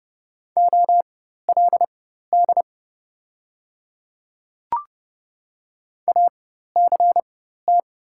Morse code, heard as a keyed tone, about 30 wpm, spaced as for 14 wpm, 700 Hz. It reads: OLD ACT
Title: 4x Character Spacing - Top 300 Words - 30wpm